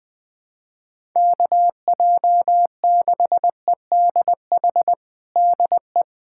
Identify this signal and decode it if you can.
Morse code, heard as a keyed tone, about 21 wpm, 700 Hz. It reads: KJ6EDH DE
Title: USA Callsign DE Callsign - 6 Characters - 20wpm